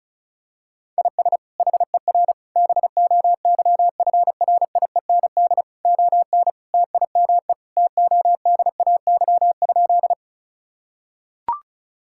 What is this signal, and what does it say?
Morse code, heard as a keyed tone, about 36 wpm, 700 Hz. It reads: IS HER BOYFRIEND ON TIME TODAY?
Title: Sentences from Top 1000 Words - Review Part 5 - 35wpm